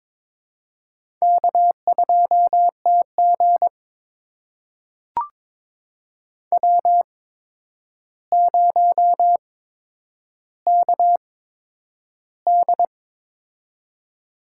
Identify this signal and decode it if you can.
Morse code, heard as a keyed tone, about 23 wpm, spaced as for 10 wpm, 700 Hz. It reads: K2TG W0KD
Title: USA Call Signs - 8x Character Spacing - 22wpm